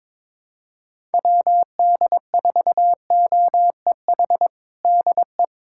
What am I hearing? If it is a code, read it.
Morse code, heard as a keyed tone, about 22 wpm, 700 Hz. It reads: WD4OEH DE
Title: USA Callsign DE Callsign - 6 Characters - 22wpm